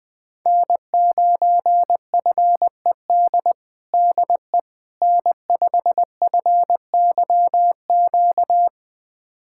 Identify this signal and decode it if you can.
Morse code, heard as a keyed tone, about 20 wpm, 700 Hz. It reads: N9FED DE N5FYQ
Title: USA Callsign DE Callsign - 5 Characters - 20wpm